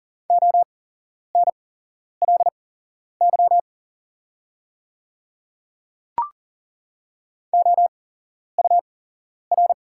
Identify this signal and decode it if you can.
Morse code, heard as a keyed tone, about 40 wpm, spaced as for 11 wpm, 700 Hz. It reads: ONLY OUR
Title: 8x Character Spacing - Top 300 Words - 40wpm